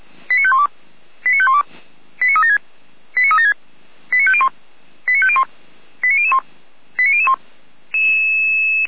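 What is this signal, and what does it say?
Five-tone selective call (ZVEI), heard as quick runs of five tones, tones of about 70 ms. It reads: 89512 89512 89376 89376 89501 89501 89001 89001
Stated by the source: ZVEI five-tone selective call received by SDR.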